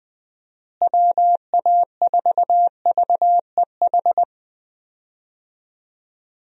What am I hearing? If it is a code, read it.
Morse code, heard as a keyed tone, about 20 wpm, 700 Hz. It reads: WA4VEH
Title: USA Call Signs - 6 Characters - Repeated 2x - 20wpm